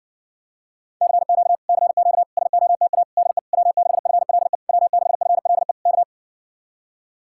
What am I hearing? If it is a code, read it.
Morse code, heard as a keyed tone, about 60 wpm, 700 Hz. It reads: CQ CQ SOTA DE W6FLE W6FLE K